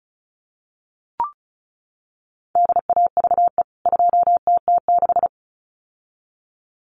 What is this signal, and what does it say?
Morse code, heard as a keyed tone, about 34 wpm, 700 Hz. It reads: DAVE 2TT6
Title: CWT Exchange 35wpm